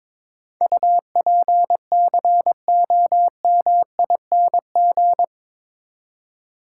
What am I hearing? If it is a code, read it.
Morse code, heard as a keyed tone, about 21 wpm, 700 Hz. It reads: UPCOMING
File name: practice / 22wpm / Eight Letter Words - 22wpm